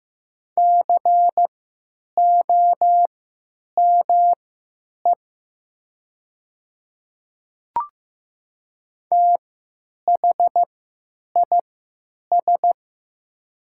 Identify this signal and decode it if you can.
Morse code, heard as a keyed tone, about 15 wpm, spaced as for 9 wpm, 700 Hz. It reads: COME THIS
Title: 3x Character Spacing - Top 100 Words - 15wpm